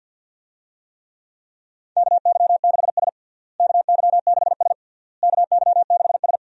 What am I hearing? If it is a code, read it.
Morse code, heard as a keyed tone, about 52 wpm, 700 Hz. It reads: KY6S KY6S KY6S
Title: USA Call Signs - 4 Characters - Repeated 3x - 50wpm